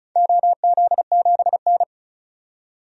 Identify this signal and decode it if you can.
Morse code, heard as a keyed tone, about 36 wpm, 700 Hz. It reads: OZ7N
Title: DXCC-100 Call Signs - 4 Characters - 35wpm